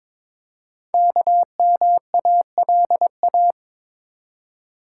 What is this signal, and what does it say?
Morse code, heard as a keyed tone, about 22 wpm, 700 Hz. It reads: KMALA
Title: Code Groups - 5 Letters - 22wpm